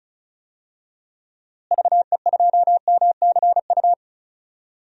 Morse code, heard as a keyed tone, about 36 wpm, 700 Hz. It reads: VE2MCU
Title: DXCC-100 Call Signs - 6 Characters - Repeated 3x - 35wpm